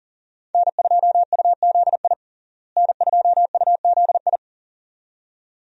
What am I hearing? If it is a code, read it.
Morse code, heard as a keyed tone, about 40 wpm, 700 Hz. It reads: N2UZI N2UZI
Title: USA Call Signs - 5 Characters - Repeated 2x - 40wpm